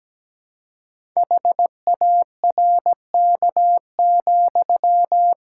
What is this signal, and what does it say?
Morse code, heard as a keyed tone, about 17 wpm, 700 Hz. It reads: HARK,
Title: Pangram Challenge v2 - 17wpm